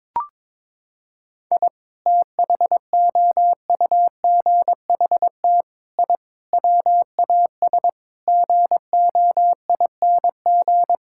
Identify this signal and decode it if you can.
Morse code, heard as a keyed tone, about 22 wpm, 700 Hz. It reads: I THOUGHT I WAS GOING
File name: practice / 22wpm / Sets of 6 Words - 22wpm